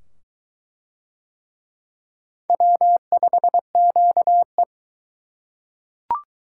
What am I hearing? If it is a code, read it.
Morse code, heard as a keyed tone, about 22 wpm, 700 Hz. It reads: W5QE